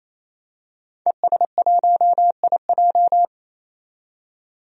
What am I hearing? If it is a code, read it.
Morse code, heard as a keyed tone, about 27 wpm, 700 Hz. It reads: ES1IJ
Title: DXCC-100 Call Signs - 5 Characters - 28wpm